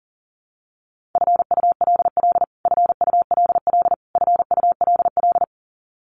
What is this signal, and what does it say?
Morse code, heard as a keyed tone, about 40 wpm, 700 Hz. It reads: FULL FULL FULL
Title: Top 500 Words Repeated 40wpm